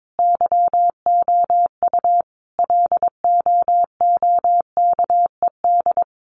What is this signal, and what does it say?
Morse code, heard as a keyed tone, about 22 wpm, 700 Hz. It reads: YOU LOOKED
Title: A Sound Of Thunder 22wpm